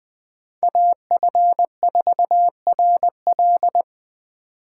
Morse code, heard as a keyed tone, about 20 wpm, 700 Hz. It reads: AF4RL
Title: Speed Racing - USA 2x2 Call Signs 17wpm